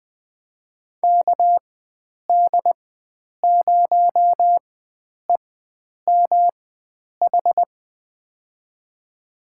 Morse code, heard as a keyed tone, about 20 wpm, spaced as for 9 wpm, 700 Hz. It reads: KD0EMH
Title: USA Call Signs - 6 Characters - 4x Character Spacing - 20wpm